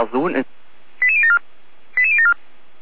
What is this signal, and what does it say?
Five-tone selective call (ZVEI), heard as quick runs of five tones, tones of about 70 ms, with speech. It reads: 88074 88074